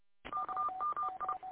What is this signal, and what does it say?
Morse code, heard as a keyed tone, about 30 wpm, 1200 Hz, with noise. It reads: KMI